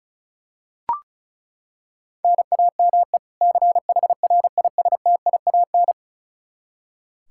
Morse code, heard as a keyed tone, about 36 wpm, 700 Hz. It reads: NAME CHRISTIAN
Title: K1USN SST QSO Practice 35wpm